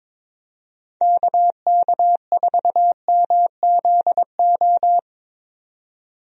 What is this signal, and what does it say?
Morse code, heard as a keyed tone, about 22 wpm, 700 Hz. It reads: KK4MZO
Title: USA Call Signs - 6 Characters - 22wpm